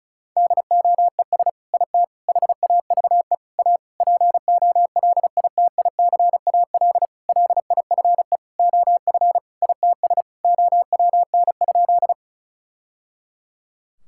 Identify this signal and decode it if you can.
Morse code, heard as a keyed tone, about 34 wpm, 690 Hz. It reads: DOES IT HAVE A POLITICAL LIFE OF ITS OWN?
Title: Sentences from Top 1000 Words - Review Part 5 - 35wpm